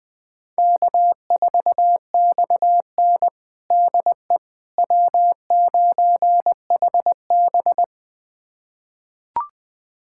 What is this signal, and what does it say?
Morse code, heard as a keyed tone, about 21 wpm, 700 Hz. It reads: K4XN DE W9HB
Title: USA Callsign DE Callsign - 4 Characters - 20wpm